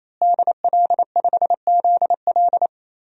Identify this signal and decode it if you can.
Morse code, heard as a keyed tone, about 27 wpm, 700 Hz. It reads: DL5ZL
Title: DXCC-100 Call Signs - 5 Characters - Repeated 3x - 28wpm